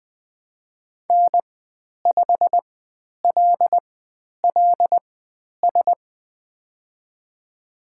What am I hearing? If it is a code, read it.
Morse code, heard as a keyed tone, about 20 wpm, spaced as for 10 wpm, 700 Hz. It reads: N5LLS
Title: 095 Lesson 16g - Letter c - Call Signs 20wpm